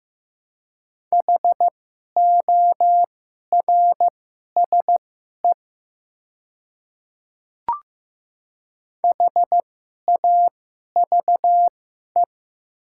Morse code, heard as a keyed tone, about 15 wpm, spaced as for 11 wpm, 700 Hz. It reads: HORSE HAVE
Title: 2x Character Spacing - Top 300 Words - 15wpm